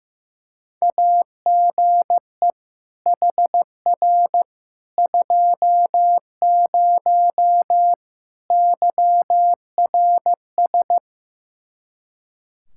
Morse code, heard as a keyed tone, about 15 wpm, 700 Hz. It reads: AGE HR 20 YRS
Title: Practice QSO - Exchange 3 - More Info - 15wpm